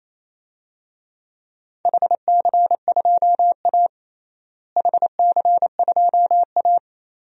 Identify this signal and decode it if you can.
Morse code, heard as a keyed tone, about 28 wpm, 700 Hz. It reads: HC2A HC2A
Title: DXCC-100 Call Signs - 4 Characters - Repeated 3x - 28wpm